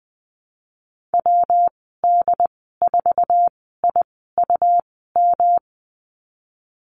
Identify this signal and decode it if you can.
Morse code, heard as a keyed tone, about 20 wpm, spaced as for 15 wpm, 700 Hz. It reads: WD4IUM
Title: 122 Lesson 20g - Number 3 - Call Signs 20wpm